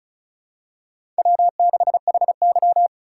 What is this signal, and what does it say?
Morse code, heard as a keyed tone, about 36 wpm, 700 Hz. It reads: W6HY
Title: USA Call Signs - 4 Characters - 2x Character Spacing - 35wpm